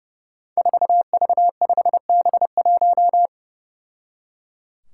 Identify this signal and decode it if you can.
Morse code, heard as a keyed tone, about 30 wpm, 700 Hz. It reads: 4V5B1
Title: Five Challenging Letter-Number Combinations 30wpm